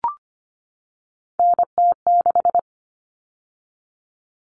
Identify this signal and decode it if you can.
Morse code, heard as a keyed tone, about 25 wpm, 700 Hz. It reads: NT6
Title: All Cut Numbers - 3-Digit 25wpm